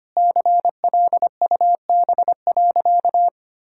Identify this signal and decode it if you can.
Morse code, heard as a keyed tone, about 25 wpm, 700 Hz. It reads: CLUB.
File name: practice / 25wpm / Sentences with 4-Word Phrases - 25wpm